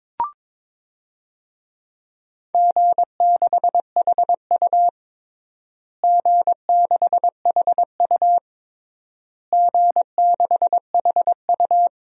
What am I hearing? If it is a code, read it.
Morse code, heard as a keyed tone, about 22 wpm, 700 Hz. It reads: G6HU G6HU G6HU
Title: DXCC-100 Call Signs - 4 Characters - Repeated 3x - 22wpm